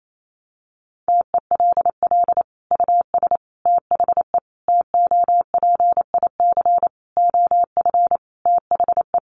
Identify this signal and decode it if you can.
Morse code, heard as a keyed tone, about 28 wpm, 700 Hz. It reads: TELL US THE TOPIC OF THE
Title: Sentences from Top 1200 Words 28wpm